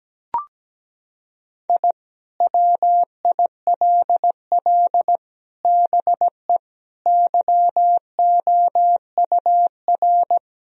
Morse code, heard as a keyed tone, about 17 wpm, 700 Hz. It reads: I WILL BE YOUR
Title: Sentences from Top 1000 Words - Review Part 4 - 17wpm